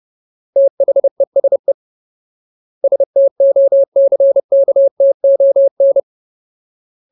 Morse code, heard as a keyed tone, about 30 wpm, 550 Hz. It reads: THESE STOCKTON
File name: Ultimate Word List / 30WPM Ultimate